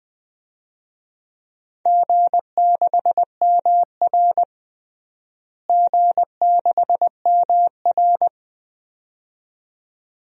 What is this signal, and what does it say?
Morse code, heard as a keyed tone, about 20 wpm, 700 Hz. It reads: G6MR G6MR
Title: DXCC-100 Call Signs - 4 Characters - Repeated 2x - 20wpm